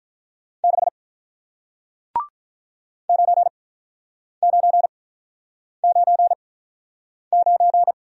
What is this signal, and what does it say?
Morse code, heard as a keyed tone, about 48 wpm, spaced as for 9 wpm, 700 Hz. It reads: 6 9999